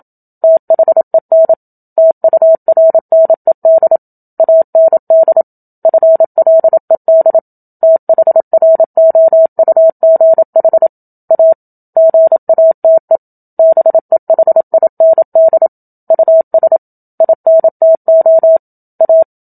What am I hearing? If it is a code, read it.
Morse code, heard as a keyed tone, about 27 wpm, 650 Hz. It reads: THEN TURNED AND FLED THROUGH A GATE BEHIND US INTO A
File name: Book Two / 01-Chapter 1 - 27 WPM